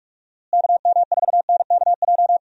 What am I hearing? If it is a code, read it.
Morse code, heard as a keyed tone, about 41 wpm, 700 Hz. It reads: KM4NKJ
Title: USA Call Signs - 6 Characters - Repeated 2x - 45wpm